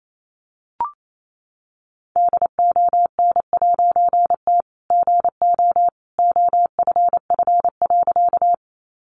Morse code, heard as a keyed tone, about 28 wpm, 700 Hz. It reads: DON'T GO OFF.
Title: A Sound Of Thunder 28wpm